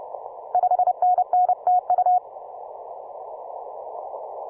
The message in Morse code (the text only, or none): 5NN TU